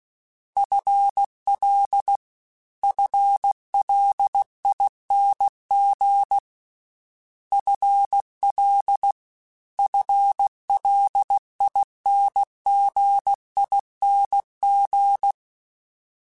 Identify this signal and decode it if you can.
Morse code, heard as a keyed tone, about 16 wpm, 800 Hz. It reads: FL FLING FL FLINGING
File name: FL / FL mp3s / FL Prefixes 1x1 @15